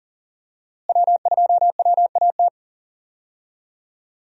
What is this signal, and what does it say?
Morse code, heard as a keyed tone, about 40 wpm, 700 Hz. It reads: W2WAT
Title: USA Call Signs - 5 Characters - Repeated 3x - 40wpm